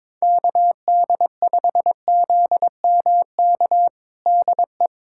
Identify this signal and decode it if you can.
Morse code, heard as a keyed tone, about 22 wpm, 700 Hz. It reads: KD5ZMK DE
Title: USA Callsign DE Callsign - 6 Characters - 22wpm